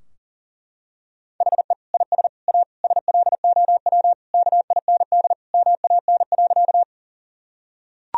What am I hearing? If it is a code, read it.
Morse code, heard as a keyed tone, about 40 wpm, 700 Hz. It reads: HE IS A SLOW KIND MAN.
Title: Sentences from Top 1000 Words - Review Part 2 - 40wpm